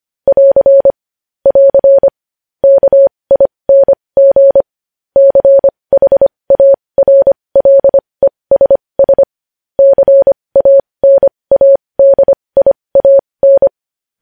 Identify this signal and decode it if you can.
Morse code, heard as a keyed tone, about 25 wpm, spaced as for 20 wpm, 550 Hz. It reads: <AR> <AR> KING CHARLESS CANADIAN
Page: News Headlines in Morse Code at 20 WPM